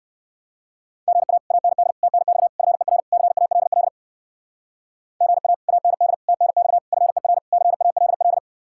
Morse code, heard as a keyed tone, about 62 wpm, 700 Hz. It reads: GA AND TNX FER CALL GA AND TNX FER CALL